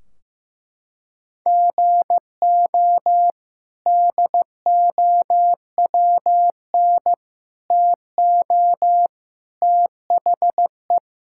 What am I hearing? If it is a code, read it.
Morse code, heard as a keyed tone, about 15 wpm, 700 Hz. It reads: GO DOWN TO THE
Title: Sentences from Top 1600 Words 15wpm